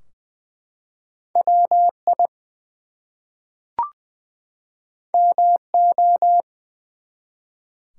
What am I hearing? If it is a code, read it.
Morse code, heard as a keyed tone, about 20 wpm, 700 Hz. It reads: WI MO
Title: US States and Canadian Provinces 20wpm